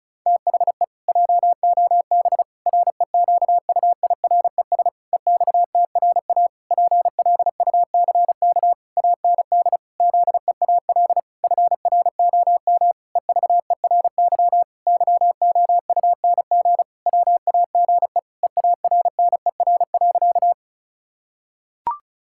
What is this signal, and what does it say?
Morse code, heard as a keyed tone, about 36 wpm, 700 Hz. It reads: THE JOB REQUIRES EXTRA PLUCK AND ZEAL FROM EVERY YOUNG WAGE EARNER.
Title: Pangram Challenge v2 - 35wpm